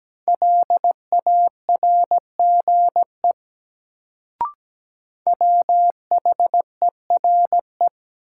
Morse code, heard as a keyed tone, about 17 wpm, 700 Hz. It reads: LARGE WHERE
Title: Sentences from Top 200 Words - New Word Drill - 17wpm